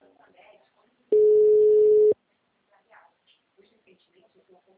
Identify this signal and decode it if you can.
Ringback tone (Europe, 425 Hz), with speech.
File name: out- -2006- - - .81475